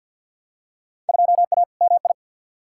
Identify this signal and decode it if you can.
Morse code, heard as a keyed tone, about 46 wpm, 700 Hz. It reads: 2A MI
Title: Field Day Exchanges - 50wpm